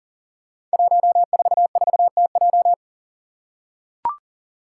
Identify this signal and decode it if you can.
Morse code, heard as a keyed tone, about 39 wpm, 700 Hz. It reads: 144TJ
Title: Five Challenging Letter-Number Combinations 40wpm